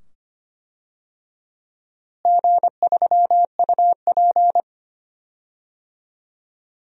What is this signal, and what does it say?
Morse code, heard as a keyed tone, about 25 wpm, 700 Hz. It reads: G3UP
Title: DXCC-100 Call Signs - 4 Characters - Repeated 2x - 25wpm